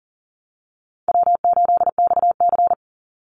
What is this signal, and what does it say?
Morse code, heard as a keyed tone, about 40 wpm, 700 Hz. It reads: W8XC
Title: Easy Call Signs - 40wpm